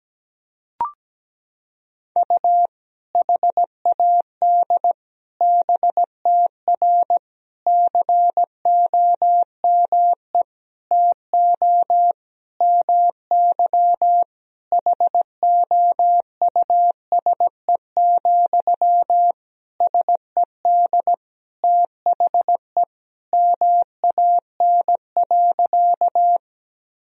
Morse code, heard as a keyed tone, about 17 wpm, 700 Hz. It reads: U HAD BTR COME TO MY HOUSE, SED THE MAN.
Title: Sentences from Top 400 Words and Abbreviations 17wpm